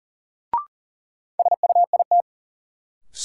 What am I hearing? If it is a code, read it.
Morse code, heard as a keyed tone, about 40 wpm, 700 Hz. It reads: SUIT
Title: Sentences from Top 1600 Words - New Word Drill - 40wpm